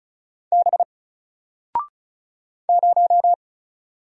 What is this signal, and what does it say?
Morse code, heard as a keyed tone, about 33 wpm, 700 Hz. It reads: B 0
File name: Single-Letter-Number-35wpm